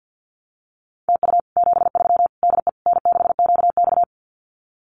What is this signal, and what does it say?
Morse code, heard as a keyed tone, about 50 wpm, 700 Hz. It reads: TU 73 DE N6QX
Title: Common QSO Phrases Encore 50wpm